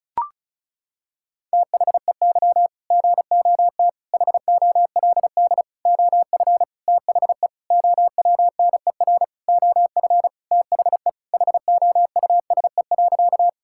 Morse code, heard as a keyed tone, about 36 wpm, 700 Hz. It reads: THEY GOT HOLD OF THE OWNER OF THE HOUSE.
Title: Sentences from Top 1600 Words 35wpm